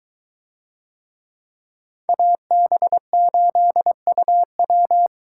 Morse code, heard as a keyed tone, about 22 wpm, 700 Hz. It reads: AB8UW